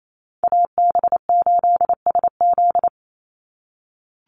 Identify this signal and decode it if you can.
Morse code, heard as a keyed tone, about 28 wpm, 700 Hz. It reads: AB8SZ
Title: USA Call Signs 28wpm